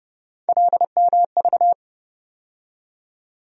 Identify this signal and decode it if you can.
Morse code, heard as a keyed tone, about 30 wpm, 700 Hz. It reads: LMV
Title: USA Call Sign Postfixes - 3 Letters 30wpm